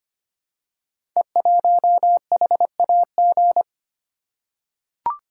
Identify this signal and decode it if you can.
Morse code, heard as a keyed tone, about 25 wpm, 700 Hz. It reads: E1HAG
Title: Code Groups - 5 Letter:Numbers - 25wpm